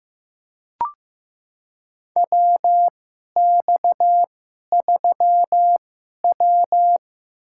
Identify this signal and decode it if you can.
Morse code, heard as a keyed tone, about 15 wpm, spaced as for 11 wpm, 700 Hz. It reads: WX3W
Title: USA Call Signs - 4 Characters - 2x Character Spacing - 15wpm